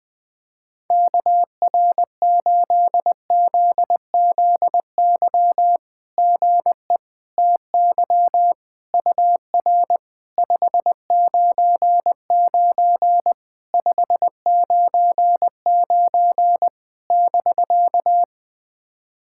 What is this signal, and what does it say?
Morse code, heard as a keyed tone, about 20 wpm, 690 Hz. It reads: KR8ZZY GE TY UR 599 599 <BK>
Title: SOTA QSO Practice 20wpm